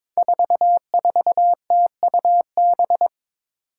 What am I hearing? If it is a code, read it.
Morse code, heard as a keyed tone, about 22 wpm, 700 Hz. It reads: 44TUB
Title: Five Challenging Letter-Number Combinations 22wpm